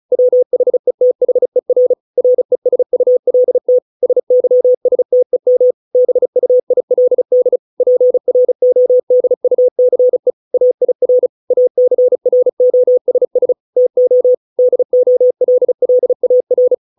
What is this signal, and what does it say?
Morse code, heard as a keyed tone, about 34 wpm, 500 Hz. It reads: WHETHER RESULT SYSTEM BUILD PRODUCE AIR ACROSS TO DOLLAR